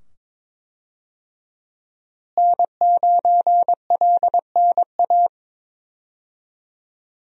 Morse code, heard as a keyed tone, about 22 wpm, 700 Hz. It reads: N9LNA
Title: USA Call Signs - 5 Characters - Repeated 2x - 22wpm